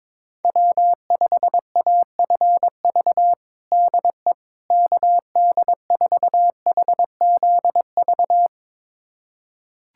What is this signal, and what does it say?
Morse code, heard as a keyed tone, about 22 wpm, 700 Hz. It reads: W5AFV DE KD4HZV
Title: USA Callsign DE Callsign 22wpm